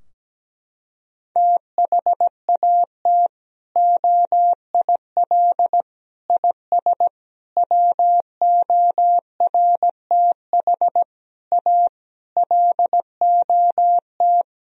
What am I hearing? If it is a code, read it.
Morse code, heard as a keyed tone, about 17 wpm, 700 Hz. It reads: THAT OIL IS WORTH A LOT
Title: Sentences from Top 1000 Words - Review Part 4 - 17wpm